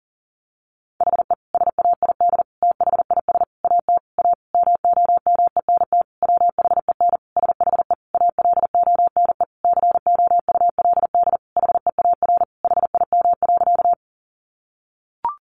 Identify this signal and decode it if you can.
Morse code, heard as a keyed tone, about 40 wpm, 700 Hz. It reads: HE SAID THIS AT A MOMENT WHEN SHE ALONE COULD HEAR HIM.
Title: Sentences from Top 1000 Words 40wpm